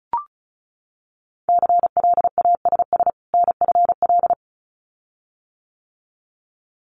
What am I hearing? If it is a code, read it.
Morse code, heard as a keyed tone, about 36 wpm, 700 Hz. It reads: CLASS NFL
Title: Winter Field Day Exchanges 35wpm